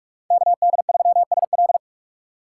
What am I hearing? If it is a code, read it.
Morse code, heard as a keyed tone, about 46 wpm, 700 Hz. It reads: KD3SL
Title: USA Call Signs - 5 Characters - 4x Character Spacing - 45wpm